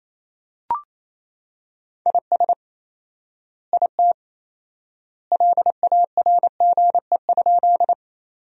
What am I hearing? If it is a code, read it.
Morse code, heard as a keyed tone, about 27 wpm, 700 Hz. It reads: IS IT LARGE?
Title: 4x Word-Spacing - Sentences from Top 200 Words - 28wpm